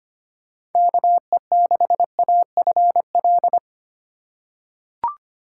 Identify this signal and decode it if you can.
Morse code, heard as a keyed tone, about 25 wpm, 700 Hz. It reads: KE6AFL
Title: USA Call Signs - 6 Characters - 2x Character Spacing - 25wpm